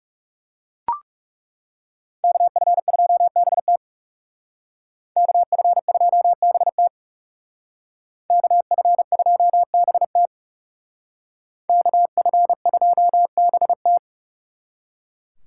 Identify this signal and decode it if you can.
Morse code, heard as a keyed tone, about 36 wpm, 700 Hz. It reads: KF2BT KF2BT KF2BT KF2BT
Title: Speed Racing - USA 2x2 Call Signs 30wpm